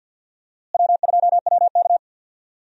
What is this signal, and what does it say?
Morse code, heard as a keyed tone, about 51 wpm, 700 Hz. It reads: W2WK
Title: USA Call Signs - 4 Characters - Repeated 3x - 50wpm